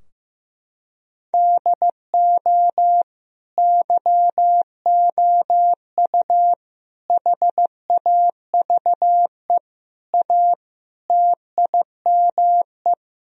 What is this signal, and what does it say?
Morse code, heard as a keyed tone, about 15 wpm, 700 Hz. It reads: DO YOU HAVE A TIME
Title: Sentences from Top 1000 Words - Review Part 2 - 15wpm